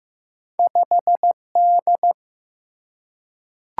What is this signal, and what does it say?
Morse code, heard as a keyed tone, about 15 wpm, 700 Hz. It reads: 5D
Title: Single-Letter Single-Number 15wpm